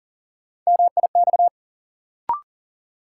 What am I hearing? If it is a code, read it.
Morse code, heard as a keyed tone, about 40 wpm, 700 Hz. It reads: MIX
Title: Sentences from Top 1000 Words - Review Part 4 - New Word Drill - 40wpm